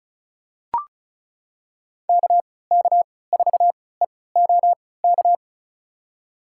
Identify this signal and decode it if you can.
Morse code, heard as a keyed tone, about 36 wpm, spaced as for 20 wpm, 690 Hz. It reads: KK4EOK
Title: USA Call Signs - 3x Character Spacing - 35wpm